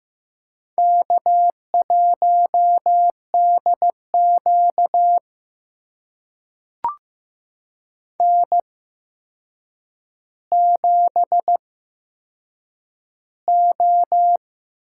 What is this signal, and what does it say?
Morse code, heard as a keyed tone, about 15 wpm, spaced as for 7 wpm, 700 Hz. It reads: K1DQ N7O
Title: USA Call Signs - 4 Characters - 8x Character Spacing - 15wpm